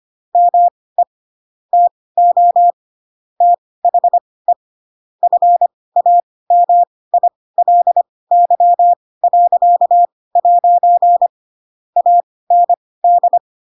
Morse code, heard as a keyed tone, about 25 wpm, spaced as for 18 wpm, 700 Hz. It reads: ME TO THE FAMILY.' AND
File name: Pinocchio / Pinocchio ch7 12 25 ik1hge